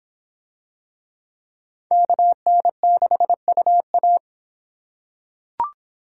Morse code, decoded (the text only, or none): KN6UA